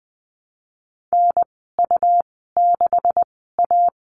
Morse code, heard as a keyed tone, about 20 wpm, spaced as for 14 wpm, 700 Hz. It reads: NU6A
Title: 154 Lesson 25g - Letter p - Call Signs 20wpm